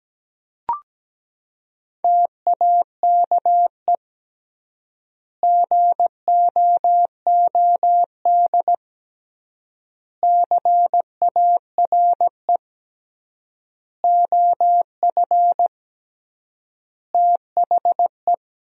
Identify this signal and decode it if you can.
Morse code, heard as a keyed tone, about 17 wpm, 700 Hz. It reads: TAKE GOOD CARE OF THE
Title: 3x Word-Spacing - Sentences from Top 400 Words - 17wpm